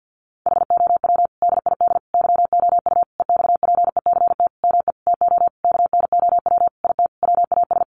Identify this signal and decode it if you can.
Morse code, heard as a keyed tone, about 50 wpm, 700 Hz. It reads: HOW DID YOU EXPECT ME TO KNOW IT WAS
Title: Sentences from Top 1100 Words 50wpm